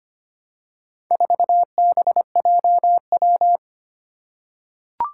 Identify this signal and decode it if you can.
Morse code, heard as a keyed tone, about 25 wpm, 700 Hz. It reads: 4BJW
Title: Four Challenging Letter-Number Combinations 25wpm